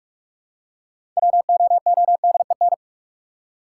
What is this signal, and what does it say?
Morse code, heard as a keyed tone, about 47 wpm, 700 Hz. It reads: WOODEN
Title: Six Letter Words - 45wpm